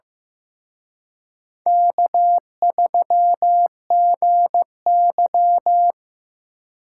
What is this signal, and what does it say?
Morse code, heard as a keyed tone, about 18 wpm, 700 Hz. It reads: K3GY
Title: Speed Racing - USA Call Signs 15wpm